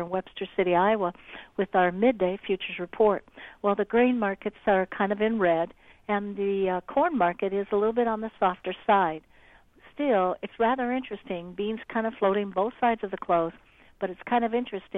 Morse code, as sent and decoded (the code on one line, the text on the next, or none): none